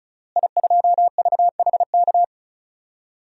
Morse code, heard as a keyed tone, about 36 wpm, 700 Hz. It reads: I2VHK
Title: DXCC-100 Call Signs - 5 Characters - Repeated 3x - 35wpm